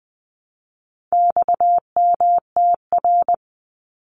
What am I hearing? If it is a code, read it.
Morse code, heard as a keyed tone, about 20 wpm, 700 Hz. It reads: XMTR
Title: Abbreviations 20wpm